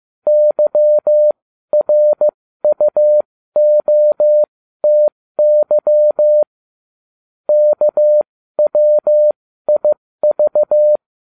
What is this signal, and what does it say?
Morse code, heard as a keyed tone, about 15 wpm, spaced as for 13 wpm, 600 Hz. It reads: YRUOTY KWIV
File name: cw / ubacw20 2